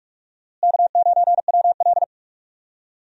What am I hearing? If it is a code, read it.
Morse code, heard as a keyed tone, about 42 wpm, 700 Hz. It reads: K9WL
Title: USA Call Signs - 4 Characters - Repeated 2x - 45wpm